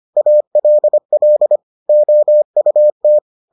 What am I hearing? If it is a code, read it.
Morse code, heard as a keyed tone, about 25 wpm, 600 Hz. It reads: ALL OUT